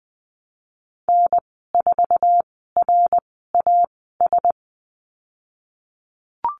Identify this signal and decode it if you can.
Morse code, heard as a keyed tone, about 20 wpm, spaced as for 15 wpm, 700 Hz. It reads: N4RAS
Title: 167 Lesson 27g - Number 7 - Call Signs 20wpm